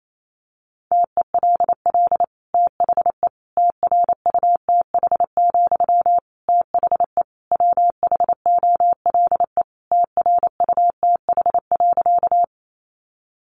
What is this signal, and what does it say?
Morse code, heard as a keyed tone, about 28 wpm, 700 Hz. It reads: TELL THE TRUTH, THE WHOLE TRUTH.
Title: Sentences from Top 800 Words 28wpm